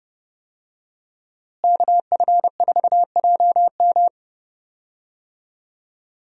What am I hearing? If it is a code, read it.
Morse code, heard as a keyed tone, about 30 wpm, 700 Hz. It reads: KF4JM
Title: USA Call Signs - 5 Characters - Repeated 3x - 30wpm